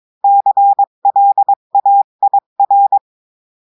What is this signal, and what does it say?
Morse code, heard as a keyed tone, about 22 wpm, 800 Hz. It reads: CLAIR
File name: cours-du-mercredi / audios / cours-853-du-2-mai-2024-etxte chiffre et texte clair-a-20-wpm